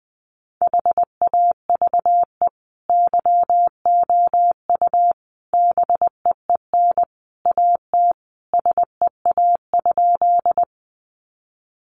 Morse code, heard as a keyed tone, about 20 wpm, 700 Hz. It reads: HAVE YOU BEEN AT SEA?
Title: Sentences from Top 300 Words 20wpm